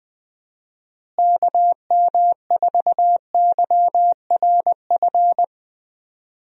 Morse code, heard as a keyed tone, about 20 wpm, 700 Hz. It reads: KM4YRF
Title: USA Call Signs - 6 Characters - Repeated 3x - 20wpm